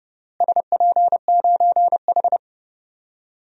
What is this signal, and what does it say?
Morse code, heard as a keyed tone, about 30 wpm, 700 Hz. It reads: SP9H
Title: DXCC-100 Call Signs - 4 Characters - Repeated 3x - 30wpm